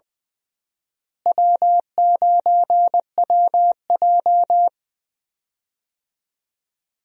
Morse code, heard as a keyed tone, about 20 wpm, 700 Hz. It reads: W9WJ
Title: USA Call Signs - 4 Characters - Repeated 3x - 20wpm